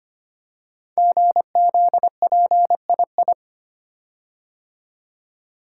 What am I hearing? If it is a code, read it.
Morse code, heard as a keyed tone, about 25 wpm, 700 Hz. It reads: GZPII
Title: Code Groups - 5 Letter:Numbers - 25wpm